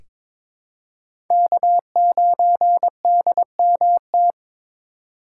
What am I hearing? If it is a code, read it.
Morse code, heard as a keyed tone, about 22 wpm, 700 Hz. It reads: K9DMT
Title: USA Call Signs - 5 Characters - 2x Character Spacing - 22wpm